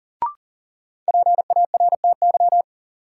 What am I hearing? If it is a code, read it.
Morse code, heard as a keyed tone, about 40 wpm, 700 Hz. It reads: PARTY
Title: Sentences from Top 1400 Words - New Word Drill - 40wpm